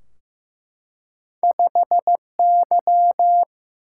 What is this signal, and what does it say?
Morse code, heard as a keyed tone, about 15 wpm, 700 Hz. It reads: 5Y
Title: Single-Letter Single-Number 15wpm